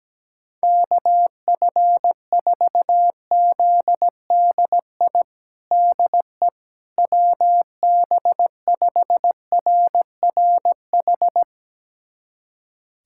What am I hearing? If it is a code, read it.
Morse code, heard as a keyed tone, about 17 wpm, 700 Hz. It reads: KF4ZDI DE WB5RRH
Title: USA Callsign DE Callsign - 6 Characters - 17wpm